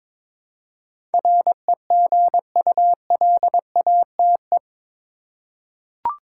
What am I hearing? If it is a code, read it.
Morse code, heard as a keyed tone, about 21 wpm, 700 Hz. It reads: REGULATE
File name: practice / 22wpm / Eight Letter Words - 22wpm